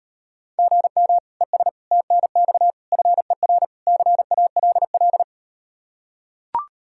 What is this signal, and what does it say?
Morse code, heard as a keyed tone, about 38 wpm, 700 Hz. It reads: GM ES TNX FER CALL